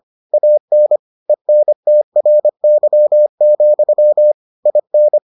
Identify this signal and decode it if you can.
Morse code, heard as a keyed tone, about 25 wpm, 600 Hz. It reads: AN ENTRY, IN